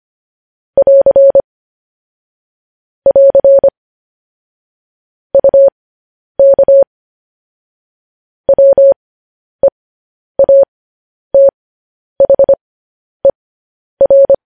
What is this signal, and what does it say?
Morse code, heard as a keyed tone, about 25 wpm, spaced as for 10 wpm, 550 Hz. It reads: <AR> <AR> UK WEATHER
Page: News Headlines in Morse Code at 10 WPM